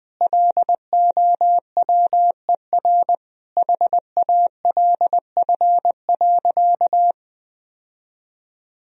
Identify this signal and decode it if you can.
Morse code, heard as a keyed tone, about 21 wpm, 700 Hz. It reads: LOWER HALF.
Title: Sentences from Top 1000 Words - Review Part 4 - 20wpm